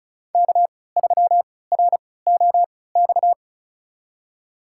Morse code, heard as a keyed tone, about 36 wpm, spaced as for 20 wpm, 700 Hz. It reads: K3ROX
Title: USA Call Signs - 3x Character Spacing - 35wpm